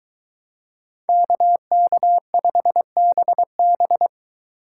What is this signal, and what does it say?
Morse code, heard as a keyed tone, about 22 wpm, 700 Hz. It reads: KK5BB